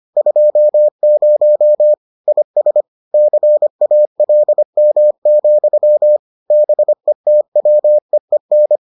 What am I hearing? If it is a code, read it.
Morse code, heard as a keyed tone, about 25 wpm, 600 Hz. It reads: 20 IS CALM, BETWEEN